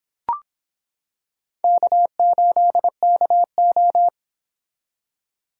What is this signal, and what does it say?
Morse code, heard as a keyed tone, about 26 wpm, 700 Hz. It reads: K8KO